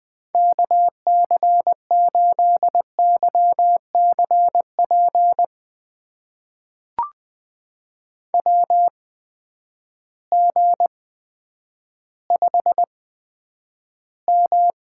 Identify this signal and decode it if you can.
Morse code, heard as a keyed tone, about 20 wpm, spaced as for 10 wpm, 700 Hz. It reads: KC8YCP WG5M
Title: USA Call Signs - 8x Character Spacing - 20wpm